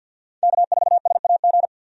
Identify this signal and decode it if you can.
Morse code, heard as a keyed tone, about 50 wpm, 700 Hz. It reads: K4SAG
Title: USA Call Signs - 5 Characters - 50wpm